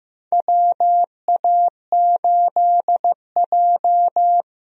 Morse code, heard as a keyed tone, about 15 wpm, 700 Hz. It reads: WA8J
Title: USA Call Signs - 4 Characters - 15wpm